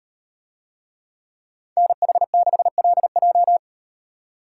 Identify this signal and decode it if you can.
Morse code, heard as a keyed tone, about 38 wpm, 700 Hz. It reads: NH6LJ